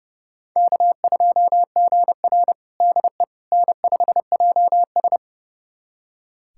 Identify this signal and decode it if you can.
Morse code, heard as a keyed tone, about 30 wpm, 700 Hz. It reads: K2GR DE N5JS
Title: USA Callsign DE Callsign - 4 Characters - 30wpm